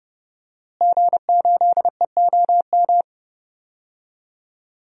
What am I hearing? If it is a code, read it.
Morse code, heard as a keyed tone, about 30 wpm, 700 Hz. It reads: G8EOM
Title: DXCC-100 Call Signs - 5 Characters - Repeated 2x - 30wpm